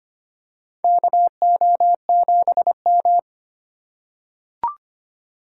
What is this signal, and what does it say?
Morse code, heard as a keyed tone, about 25 wpm, 700 Hz. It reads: KO7M
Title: USA Call Signs - 4 Characters - 2x Character Spacing - 25wpm